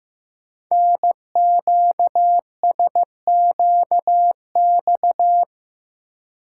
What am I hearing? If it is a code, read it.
Morse code, heard as a keyed tone, about 15 wpm, 700 Hz. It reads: NQSQX
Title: Code Groups - 5 Letters - 15wpm